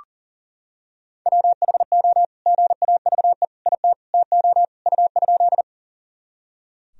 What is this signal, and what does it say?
Morse code, heard as a keyed tone, about 40 wpm, 700 Hz. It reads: WHO GAVE IT TO U?
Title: Sentences from Top 500 Words and Abbreviations 40wpm